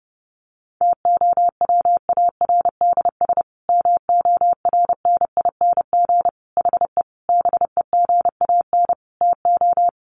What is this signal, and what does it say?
Morse code, heard as a keyed tone, about 30 wpm, 700 Hz. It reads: TOWARDS MORNING HE BEGAN TO
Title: Sentences from Top 1100 Words 30wpm